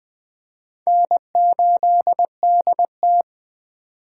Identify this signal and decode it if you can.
Morse code, heard as a keyed tone, about 21 wpm, 700 Hz. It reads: N8DT
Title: Speed Racing - USA 1x2 Call Signs 17wpm